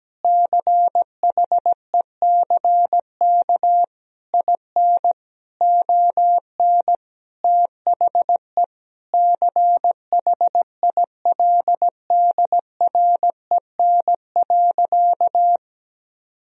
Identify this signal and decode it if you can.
Morse code, heard as a keyed tone, about 17 wpm, 700 Hz. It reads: CHECK IN ON THE CHILDREN.
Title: Sentences from Top 1000 Words - Review Part 3 - 17wpm